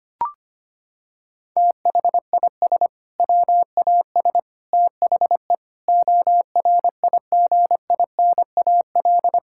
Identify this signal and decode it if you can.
Morse code, heard as a keyed tone, about 25 wpm, 700 Hz. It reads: THIS WAS THE ORIGINAL
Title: Sentences from Top 1000 Words - Review Part 6 - 25wpm